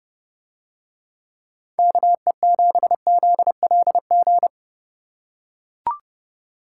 Morse code, heard as a keyed tone, about 29 wpm, 700 Hz. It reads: KE7ZLG